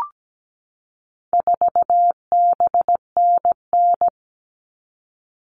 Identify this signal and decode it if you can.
Morse code, heard as a keyed tone, about 17 wpm, 700 Hz. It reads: 4BNN
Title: All Cut Numbers - 4-Digit 17wpm